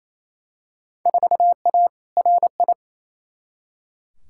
Morse code, heard as a keyed tone, about 28 wpm, 700 Hz. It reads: 4A RI
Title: Field Day Exchanges Encore - 28wpm